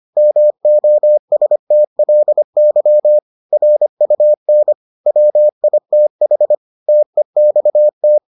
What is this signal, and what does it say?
Morse code, heard as a keyed tone, about 25 wpm, 600 Hz. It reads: MOSTLY RUN WITH TEXT